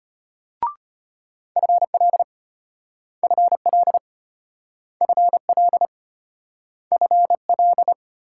Morse code, heard as a keyed tone, about 33 wpm, 700 Hz. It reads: FL FL FL FL